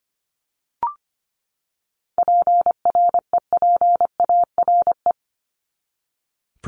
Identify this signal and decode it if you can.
Morse code, heard as a keyed tone, about 25 wpm, 700 Hz. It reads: PREPARE
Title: ED Suffix - Top 100 Words with Base Words 25wpm